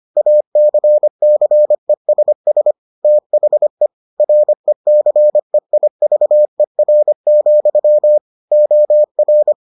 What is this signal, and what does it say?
Morse code, heard as a keyed tone, about 25 wpm, 600 Hz. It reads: ACCESS THE RECEIVER, OR